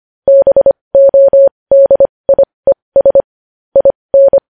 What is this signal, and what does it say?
Morse code, heard as a keyed tone, about 25 wpm, spaced as for 20 wpm, 550 Hz. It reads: BODIES IN
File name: podcast / News headlines-1x-25x20